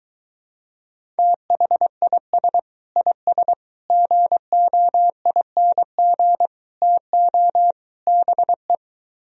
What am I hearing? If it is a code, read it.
Morse code, heard as a keyed tone, about 23 wpm, 700 Hz. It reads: THIS IS GOING TO BE